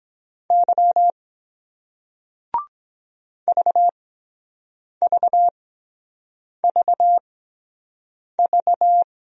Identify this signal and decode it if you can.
Morse code, heard as a keyed tone, about 23 wpm, spaced as for 7 wpm, 700 Hz. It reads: Y VVVV